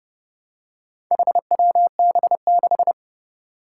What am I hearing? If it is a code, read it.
Morse code, heard as a keyed tone, about 30 wpm, 700 Hz. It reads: HWB6
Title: Four Challenging Letter-Number Combinations 30wpm